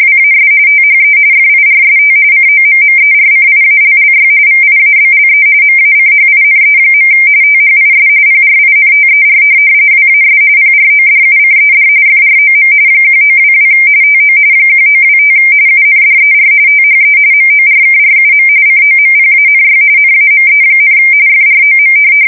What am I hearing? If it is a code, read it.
RTTY, about 45.45 baud. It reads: WAS THE FIRST COMMERCIALLY VIABLE MACHINE. THIS PRINTER WAS PRODUCED FROM 1922 TO 1925 UNDER THE MORKRUM COMPANY NAME, FROM
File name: fsk170 / teletype-corp